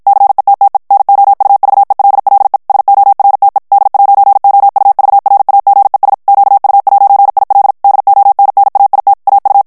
Morse code, heard as a keyed tone, about 52 wpm, 800 Hz. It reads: CETTE NOUVELLE SORTE D'OUVRAGES QU'IL DONNAIT AU
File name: FRANCAIS TEXTE CLAIR 2 VITESSE 50 MOTS MINUTES